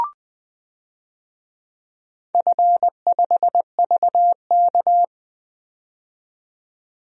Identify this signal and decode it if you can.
Morse code, heard as a keyed tone, about 20 wpm, 700 Hz. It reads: F5VK
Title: DXCC-100 Call Signs - 4 Characters - 20wpm